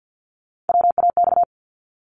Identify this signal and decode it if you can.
Morse code, heard as a keyed tone, about 50 wpm, 700 Hz. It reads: WAX
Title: Multi-letter Phonemes 50wpm